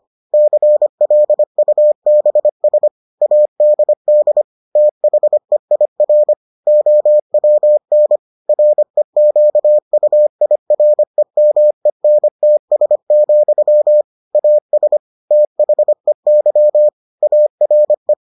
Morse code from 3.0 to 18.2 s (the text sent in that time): ADD THEIR OWN REQUIREMENTS, AS THEY ARE